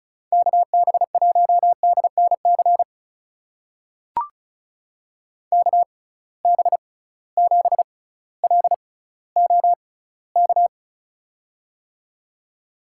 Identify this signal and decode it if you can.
Morse code, heard as a keyed tone, about 35 wpm, spaced as for 18 wpm, 700 Hz. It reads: KB1DNC KB7LOK
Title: USA Call Signs - 6 Characters - 6x Character Spacing - 35wpm